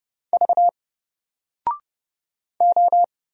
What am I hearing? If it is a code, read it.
Morse code, heard as a keyed tone, about 31 wpm, 700 Hz. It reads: V O
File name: Single-Letter-Number-30wpm